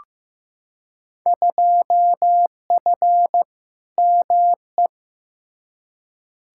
Morse code, heard as a keyed tone, about 15 wpm, 700 Hz. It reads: 2F ME
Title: Field Day Exchanges - 15wpm